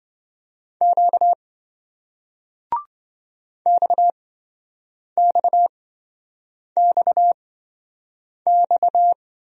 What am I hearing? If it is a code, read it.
Morse code, heard as a keyed tone, about 27 wpm, spaced as for 7 wpm, 700 Hz. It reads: Q XXXX